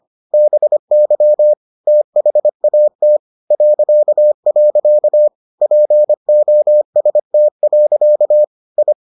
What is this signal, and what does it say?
Morse code, heard as a keyed tone, about 25 wpm, 600 Hz. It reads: BY THAT .. POST. I